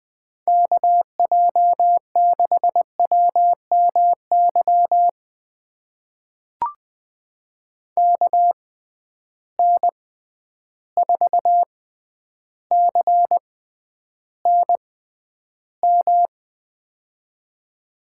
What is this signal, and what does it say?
Morse code, heard as a keyed tone, about 20 wpm, spaced as for 10 wpm, 700 Hz. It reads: KJ6WMY KN4CNM
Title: USA Call Signs - 6 Characters - 6x Character Spacing - 20wpm